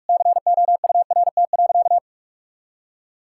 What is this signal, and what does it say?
Morse code, heard as a keyed tone, about 41 wpm, 700 Hz. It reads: COURT.
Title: Sentences from Top 1600 Words 45wpm